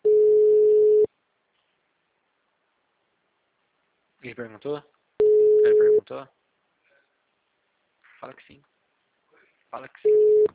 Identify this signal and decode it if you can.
Ringback tone (Europe, 425 Hz), with speech.